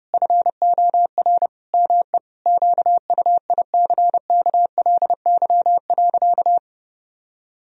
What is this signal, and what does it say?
Morse code, heard as a keyed tone, about 32 wpm, 700 Hz. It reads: FOR ME QUICKLY.
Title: Sentences from Top 1000 Words - Review Part 1 - 30wpm